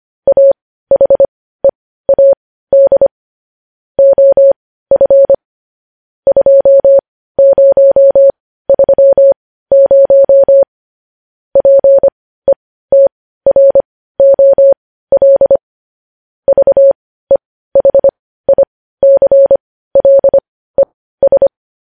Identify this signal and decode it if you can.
Morse code, heard as a keyed tone, about 25 wpm, spaced as for 15 wpm, 550 Hz. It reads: AHEAD OF 2030 PETROL VEHICLES
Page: news-headlines-in-morse-code-at-15-wpm